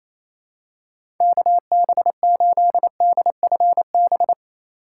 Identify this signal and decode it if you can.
Morse code, heard as a keyed tone, about 28 wpm, 700 Hz. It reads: KB8DFB
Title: USA Call Signs - 6 Characters - 4x Character Spacing - 28wpm